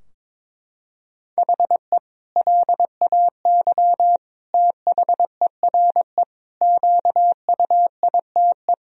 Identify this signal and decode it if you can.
Morse code, heard as a keyed tone, about 22 wpm, 700 Hz. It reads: HE LAY THERE QUITE
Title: Sentences from Top 1000 Words - Review Part 2 - 22wpm